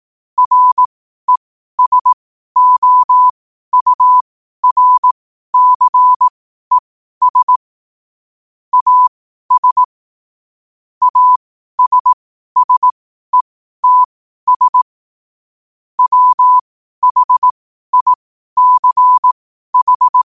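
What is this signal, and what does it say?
Morse code, heard as a keyed tone, about 18 wpm, spaced as for 12 wpm, 1000 Hz. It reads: RESOURCES AS ASSETS WHICH
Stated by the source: Morse practice sent at 12 wpm.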